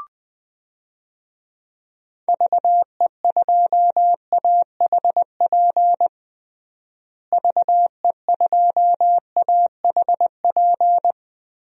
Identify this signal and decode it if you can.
Morse code, heard as a keyed tone, about 20 wpm, 700 Hz. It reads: VE2AHP VE2AHP
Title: DXCC-100 Call Signs - 6 Characters - Repeated 2x - 20wpm